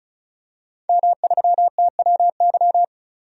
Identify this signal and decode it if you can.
Morse code, heard as a keyed tone, about 36 wpm, 700 Hz. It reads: M3TWY
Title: DXCC-100 Call Signs - 5 Characters - 35wpm